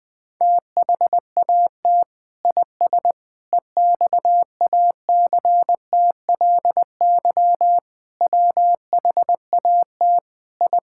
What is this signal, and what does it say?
Morse code, heard as a keyed tone, about 20 wpm, 700 Hz. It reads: THAT IS EXACTLY WHAT I
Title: Sentences from Top 1000 Words - Review Part 6 - 20wpm